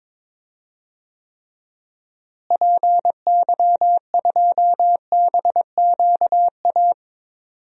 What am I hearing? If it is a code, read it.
Morse code, heard as a keyed tone, about 22 wpm, 700 Hz. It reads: PY2BQA
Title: DXCC-100 Call Signs - 6 Characters - Repeated 3x - 22wpm